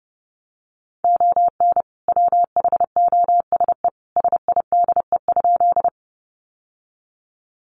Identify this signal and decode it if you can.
Morse code, heard as a keyed tone, about 30 wpm, 700 Hz. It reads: ON WHOSE SIDE?
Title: Sentences from Top 1200 Words 30wpm